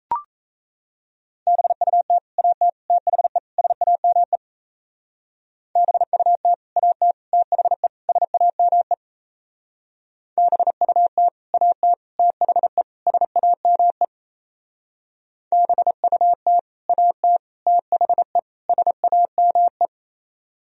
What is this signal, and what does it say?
Morse code, heard as a keyed tone, about 35 wpm, 700 Hz. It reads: BUT AT THE SAME BUT AT THE SAME BUT AT THE SAME BUT AT THE SAME